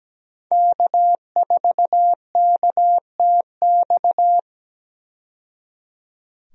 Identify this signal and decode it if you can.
Morse code, heard as a keyed tone, about 17 wpm, 700 Hz. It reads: K4KTX
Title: USA Call Signs - 5 Characters - 17wpm